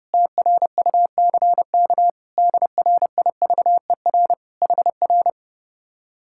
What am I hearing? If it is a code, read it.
Morse code, heard as a keyed tone, about 34 wpm, 700 Hz. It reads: TRUCK DRIVER HR
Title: Speed-Racing Common QSO Phrases Encore - 30wpm